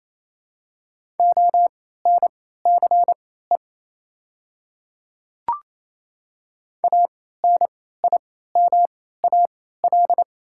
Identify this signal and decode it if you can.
Morse code, heard as a keyed tone, about 28 wpm, spaced as for 16 wpm, 700 Hz. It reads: ONCE ANIMAL
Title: 3x Character Spacing - Top 300 Words - 28wpm